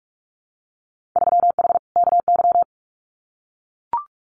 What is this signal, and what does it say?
Morse code, heard as a keyed tone, about 44 wpm, 700 Hz. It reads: 3H KY
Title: Winter Field Day Exchanges Encore 45wpm